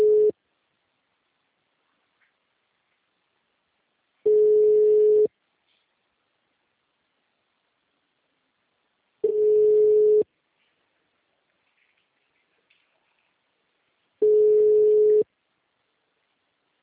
Ringback tone (Europe, 425 Hz).